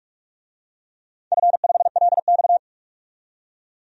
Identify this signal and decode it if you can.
Morse code, heard as a keyed tone, about 44 wpm, 700 Hz. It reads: F5LX
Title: DXCC-100 Call Signs - 4 Characters - 45wpm